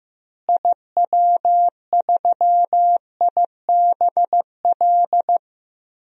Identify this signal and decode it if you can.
Morse code, heard as a keyed tone, about 15 wpm, 700 Hz. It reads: IW3IBL
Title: DXCC-100 Call Signs - 6 Characters - Repeated 3x - 15wpm